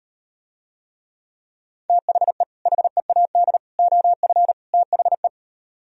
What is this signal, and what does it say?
Morse code, heard as a keyed tone, about 38 wpm, 700 Hz. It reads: THE HEAD OF THE